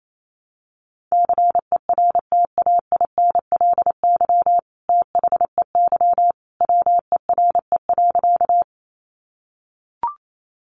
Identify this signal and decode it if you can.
Morse code, heard as a keyed tone, about 28 wpm, 700 Hz. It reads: CERTAINLY THEY WERE.
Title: Sentences from Top 900 Words 28wpm